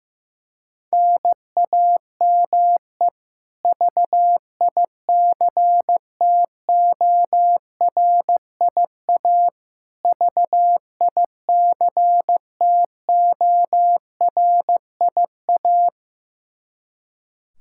Morse code, heard as a keyed tone, about 16 wpm, 700 Hz. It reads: NAME VICTORIA VICTORIA
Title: K1USN SST QSO Practice 15wpm